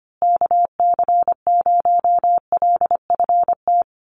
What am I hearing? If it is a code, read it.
Morse code, heard as a keyed tone, about 25 wpm, 700 Hz. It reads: KC0LFT
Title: USA Call Signs Repeated 2x 25wpm